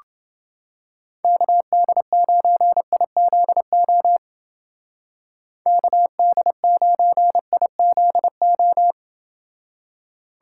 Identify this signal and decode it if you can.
Morse code, heard as a keyed tone, about 28 wpm, 700 Hz. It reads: KD9IZO KD9IZO